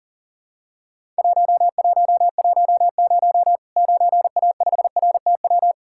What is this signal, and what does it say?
Morse code, heard as a keyed tone, about 40 wpm, 700 Hz. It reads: 1110 9A5RTW
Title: Worked All Europe DX Contest - QTC Exchange - 40wpm